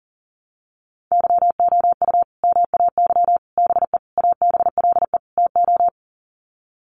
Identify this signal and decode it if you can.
Morse code, heard as a keyed tone, about 40 wpm, 700 Hz. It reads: YOU MAY BE ABLE TO
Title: Sets of 5 Words Encore 40wpm